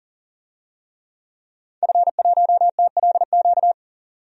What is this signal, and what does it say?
Morse code, heard as a keyed tone, about 39 wpm, 700 Hz. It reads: F1TLQ
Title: DXCC-100 Call Signs - 5 Characters - 40wpm